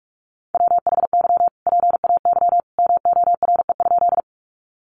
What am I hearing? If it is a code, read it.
Morse code, heard as a keyed tone, about 44 wpm, 700 Hz. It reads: WHY PAY MORE?
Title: Sentences from Top 700 Words 45wpm